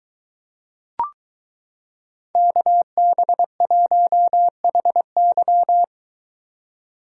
Morse code, heard as a keyed tone, about 22 wpm, 700 Hz. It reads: KB1HY